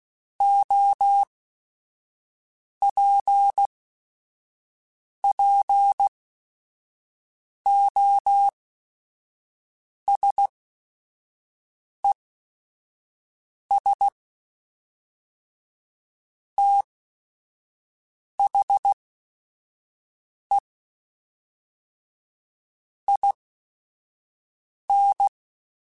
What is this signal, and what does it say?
Morse code, heard as a keyed tone, about 16 wpm, spaced as for 5 wpm, 800 Hz. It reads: OPPOSES THE IN
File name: Technical Subjects / Technical Subjects-mp3s / XL @5